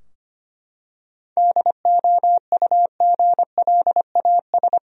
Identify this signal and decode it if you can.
Morse code, heard as a keyed tone, about 25 wpm, 700 Hz. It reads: DOUGLAS
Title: USA Common Names - 25wpm